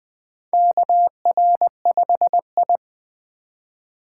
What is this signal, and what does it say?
Morse code, heard as a keyed tone, about 20 wpm, 700 Hz. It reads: KR5I
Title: USA Call Signs - 4 Characters - 8x Character Spacing - 20wpm